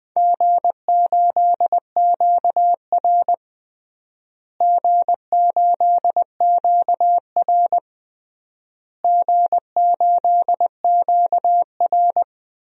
Morse code, heard as a keyed tone, about 21 wpm, 700 Hz. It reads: G8QR G8QR G8QR
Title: DXCC-100 Call Signs - 4 Characters - Repeated 3x - 20wpm